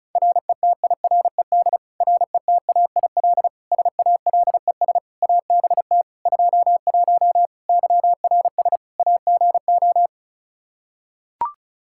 Morse code, heard as a keyed tone, about 34 wpm, 700 Hz. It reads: RETIRED RETAIL SALES ABT 21 YRS AGO
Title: Practice QSO - Exchange 3 - More Info - 35wpm